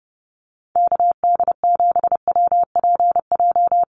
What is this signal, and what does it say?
Morse code, heard as a keyed tone, about 30 wpm, 700 Hz. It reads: KD7WPJ
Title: NA SOTA Activator Callsigns 30wpm